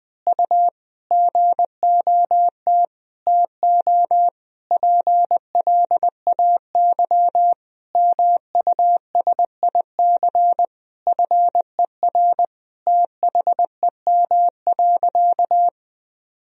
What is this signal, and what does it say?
Morse code, heard as a keyed tone, about 20 wpm, 700 Hz. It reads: U GOT TO PLAY MUSIC FER THEM.
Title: Sentences from Top 300 Words and Abbreviations 20wpm